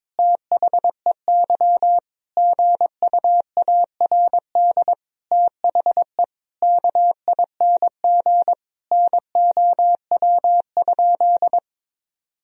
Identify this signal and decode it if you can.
Morse code, heard as a keyed tone, about 22 wpm, 700 Hz. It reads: THEY GUARD THE KING NOW?
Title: Sentences from Top 1000 Words - Review Part 4 - 22wpm